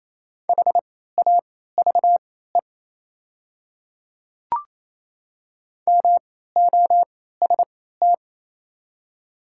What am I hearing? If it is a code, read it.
Morse code, heard as a keyed tone, about 28 wpm, spaced as for 16 wpm, 690 Hz. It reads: HAVE MOST
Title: 3x Character Spacing - Top 100 Words - 28wpm